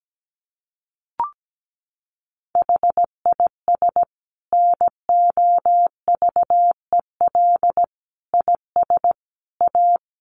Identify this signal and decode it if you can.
Morse code, heard as a keyed tone, about 17 wpm, 700 Hz. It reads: HIS NOVEL IS A
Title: Sentences from Top 1300 Words 17wpm